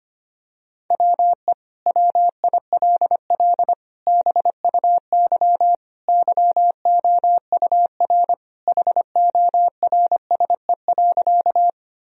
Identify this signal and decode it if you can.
Morse code, heard as a keyed tone, about 25 wpm, 700 Hz. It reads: WE WILL BUY YOUR HORSE.
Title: Sentences from Top 1000 Words - Review Part 4 - 25wpm